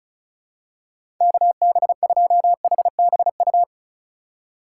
Morse code, heard as a keyed tone, about 36 wpm, 700 Hz. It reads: KB2HBU
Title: USA Call Signs - 6 Characters - 2x Character Spacing - 35wpm